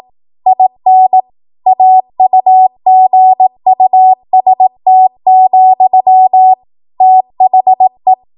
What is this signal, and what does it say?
Morse code, heard as a keyed tone, about 18 wpm, 750 Hz. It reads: IN AUGUST, THE